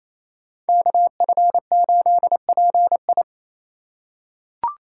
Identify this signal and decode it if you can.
Morse code, heard as a keyed tone, about 28 wpm, 700 Hz. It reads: KF8PI
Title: USA Call Signs - 5 Characters - 4x Character Spacing - 28wpm